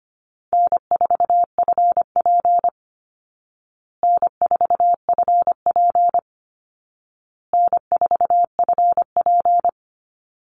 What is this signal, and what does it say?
Morse code, heard as a keyed tone, about 25 wpm, 700 Hz. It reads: N4FP N4FP N4FP
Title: Call Signs Repeated 2x - 25wpm